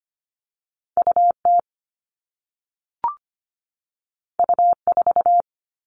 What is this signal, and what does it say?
Morse code, heard as a keyed tone, about 25 wpm, 700 Hz. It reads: UT U4
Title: All Cut Numbers - 2-Digit 25wpm